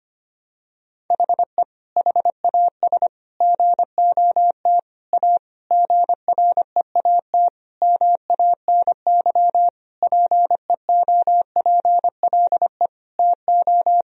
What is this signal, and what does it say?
Morse code, heard as a keyed tone, about 25 wpm, 700 Hz. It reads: HE HAS GOT A GREAT MANY PEOPLE TO
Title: Sentences from Top 1000 Words - Review Part 1 - 25wpm